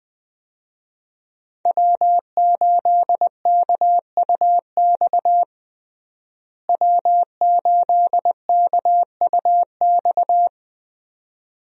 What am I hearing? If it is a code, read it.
Morse code, heard as a keyed tone, about 20 wpm, 700 Hz. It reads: W8KUX W8KUX
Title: USA Call Signs - 5 Characters - Repeated 2x - 20wpm